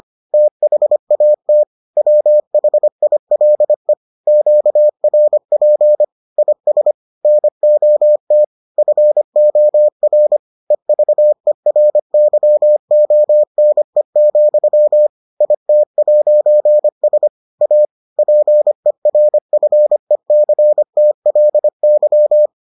Morse code, heard as a keyed tone, about 25 wpm, 600 Hz. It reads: THAT WHILE QRP IS NOT FOR EVERYONE, IT'S A PERFECTLY